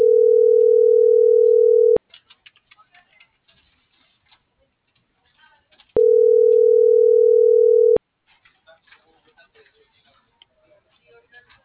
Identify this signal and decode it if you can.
Ringback tone (North America), 440 + 480 Hz.